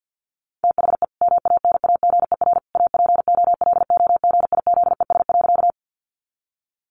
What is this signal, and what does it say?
Morse code, heard as a keyed tone, about 50 wpm, 700 Hz. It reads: THE MANAGER APOLOGIZES.
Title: Additional Easy Phrases 50wpm